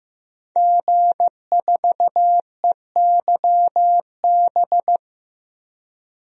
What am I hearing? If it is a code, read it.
Morse code, heard as a keyed tone, about 15 wpm, 700 Hz. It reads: G4EYB
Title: DXCC-100 Call Signs - 5 Characters - 15wpm